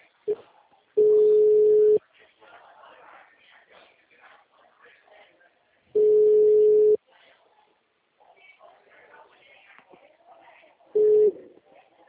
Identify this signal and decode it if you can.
Ringback tone (Europe, 425 Hz), with speech.
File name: out- -2013- - - .46159